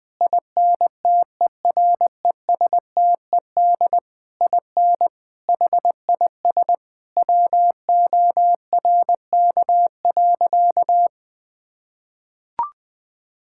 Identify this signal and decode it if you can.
Morse code, heard as a keyed tone, about 20 wpm, 700 Hz. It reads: INTERESTED IN HIS WORK.
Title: Sentences from Top 1000 Words - Review Part 6 - 20wpm